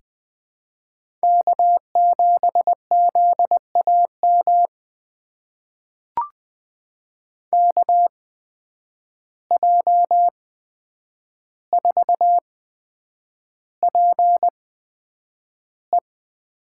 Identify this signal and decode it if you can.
Morse code, heard as a keyed tone, about 20 wpm, spaced as for 9 wpm, 700 Hz. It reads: K7ZAM KJ4PE
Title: USA Call Signs - 5 Characters - 8x Character Spacing - 20wpm